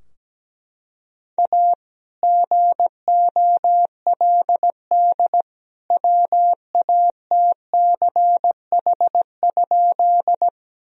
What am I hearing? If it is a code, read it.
Morse code, heard as a keyed tone, about 17 wpm, 700 Hz. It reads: A GOLD WATCH?
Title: Sentences from Top 1000 Words - Review Part 3 - 17wpm